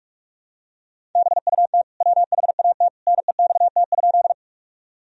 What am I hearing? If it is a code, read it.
Morse code, heard as a keyed tone, about 43 wpm, 700 Hz. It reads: BUT WHAT NEXT?
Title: Sentences from Top 1000 Words - Review Part 1 - 45wpm